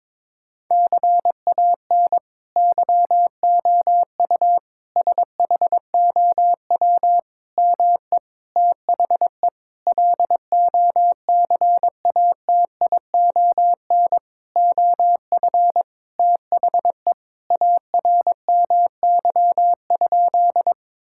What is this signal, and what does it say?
Morse code, heard as a keyed tone, about 22 wpm, 700 Hz. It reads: CAN YOU SHOW ME THE LOCATION OF THE ARMY?
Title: Sentences from Top 1000 Words - Review Part 6 - 22wpm